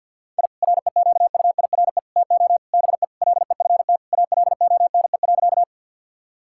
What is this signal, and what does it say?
Morse code, heard as a keyed tone, about 45 wpm, 700 Hz. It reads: I REQUIRE TO BE LEFT ALONE.
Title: Sentences from Top 1000 Words - Review Part 6 - 50wpm